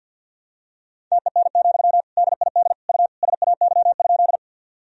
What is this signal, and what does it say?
Morse code, heard as a keyed tone, about 46 wpm, 700 Hz. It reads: TEN, DID U SAY?
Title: Sentences from Top 400 Words and Abbreviations 50wpm